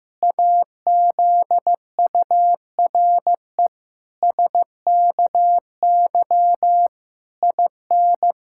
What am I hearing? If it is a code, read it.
Morse code, heard as a keyed tone, about 15 wpm, 700 Hz. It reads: AZURE SKY IN
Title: Pangram Challenge v2 - 15wpm